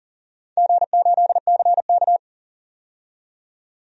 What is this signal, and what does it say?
Morse code, heard as a keyed tone, about 40 wpm, 700 Hz. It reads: G8CK
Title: DXCC-100 Call Signs - 4 Characters - 40wpm